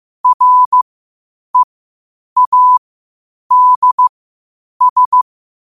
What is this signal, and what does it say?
Morse code, heard as a keyed tone, about 15 wpm, spaced as for 8 wpm, 1000 Hz. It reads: READS
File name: qotdfiles / 8wpm